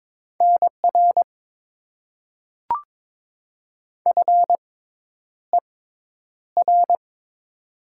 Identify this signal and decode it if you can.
Morse code, heard as a keyed tone, about 22 wpm, spaced as for 10 wpm, 700 Hz. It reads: NR FER
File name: Top 90 QSO Elements - 6x Character Spacing - 22wpm